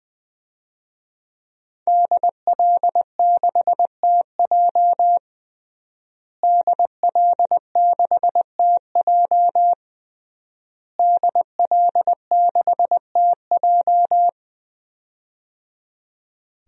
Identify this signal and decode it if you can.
Morse code, heard as a keyed tone, about 20 wpm, 700 Hz. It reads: DL6TJ DL6TJ DL6TJ
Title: DXCC-100 Call Signs - 5 Characters - Repeated 3x - 20wpm